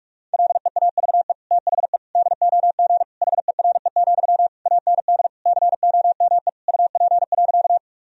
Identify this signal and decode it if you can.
Morse code, heard as a keyed tone, about 46 wpm, 700 Hz. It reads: LEAVE THE DOG HERE, AND COME UP.
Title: Sentences from Top 1000 Words - Review Part 2 - 45wpm